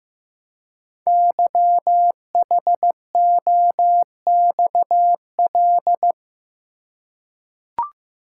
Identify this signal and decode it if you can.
Morse code, heard as a keyed tone, about 15 wpm, 700 Hz. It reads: YHOXL
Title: Code Groups - 5 Letters - 15wpm